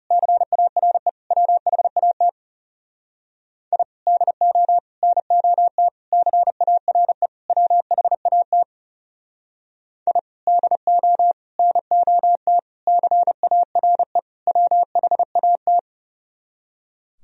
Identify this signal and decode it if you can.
Morse code, heard as a keyed tone, about 37 wpm, 700 Hz. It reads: CARE WHAT I DO NOT CARE WHAT I DO NOT CARE WHAT
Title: Speed-Racing - Sets of 5 Words 30wpm